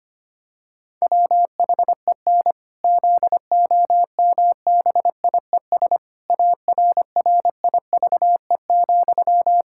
Morse code, heard as a keyed tone, about 25 wpm, 700 Hz. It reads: WHEN ZOMBIES ARRIVE,
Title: Pangram Challenge v2 - 25wpm